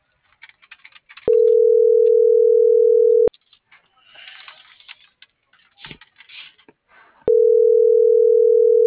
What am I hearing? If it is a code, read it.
Ringback tone (North America), 440 + 480 Hz.
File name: CENTRAL MABE / out edwinomarojeda - .71090